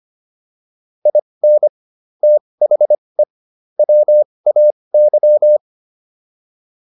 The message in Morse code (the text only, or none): IN THE WAY